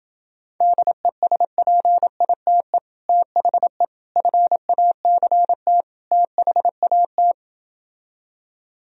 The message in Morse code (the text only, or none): DESPITE THE FACT THAT